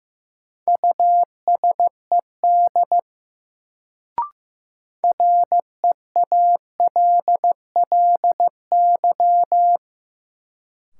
Morse code, heard as a keyed tone, about 15 wpm, 700 Hz. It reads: USED REALLY
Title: Sentences from Top 600 Words - New Word Drill - 15wpm